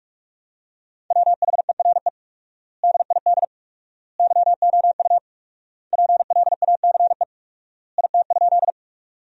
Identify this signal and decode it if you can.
Morse code, heard as a keyed tone, about 46 wpm, 700 Hz. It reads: WHERE DID YOU PLACE IT?
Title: 4x Word-Spacing - Sentences from Top 200 Words - 45wpm